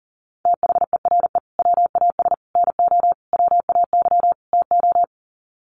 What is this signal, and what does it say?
Morse code, heard as a keyed tone, about 40 wpm, 700 Hz. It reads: THERE WAS NO WAY TO
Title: Sets of 5 Words Encore 40wpm